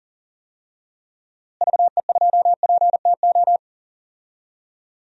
Morse code, heard as a keyed tone, about 40 wpm, 700 Hz. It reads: VE2PTO
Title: DXCC-100 Call Signs - 6 Characters - Repeated 3x - 40wpm